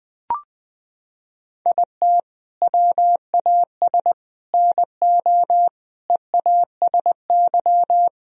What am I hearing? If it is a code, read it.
Morse code, heard as a keyed tone, about 20 wpm, 700 Hz. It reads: IT WAS NO EASY
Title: Sentences from Top 1000 Words - Review Part 3 - 20wpm